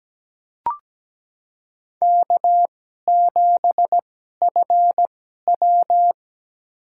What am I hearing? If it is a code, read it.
Morse code, heard as a keyed tone, about 17 wpm, spaced as for 12 wpm, 700 Hz. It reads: K7FW
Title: USA Call Signs - 4 Characters - 2x Character Spacing - 17wpm